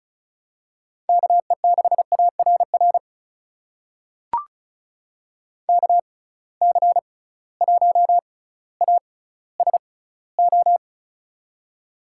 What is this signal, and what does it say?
Morse code, heard as a keyed tone, about 36 wpm, spaced as for 18 wpm, 700 Hz. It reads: KE6ARR KC1ASO
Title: USA Call Signs - 6 Characters - 6x Character Spacing - 35wpm